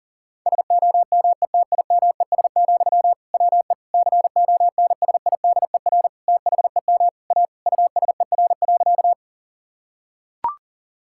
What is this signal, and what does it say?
Morse code, heard as a keyed tone, about 40 wpm, 690 Hz. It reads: SOMETIMES, WE CONSIDER THEM A USER.
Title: Sentences from Top 1000 Words - Review Part 5 - 40wpm